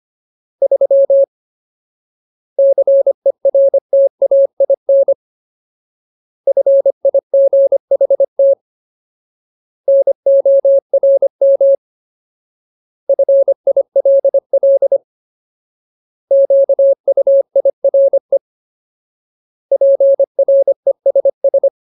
Morse code, heard as a keyed tone, about 25 wpm, 550 Hz. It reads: 3 CERTAIN FIGHT NORM FILL QUIRE PRESS
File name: Ultimate Word List / 25WPM Ultimate